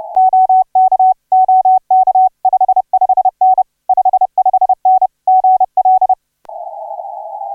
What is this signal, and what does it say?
Morse code, heard as a keyed tone, about 30 wpm, 750 Hz. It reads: OK OK 55N 55N GL
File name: RW3RN HI7 1835UTC 12m CW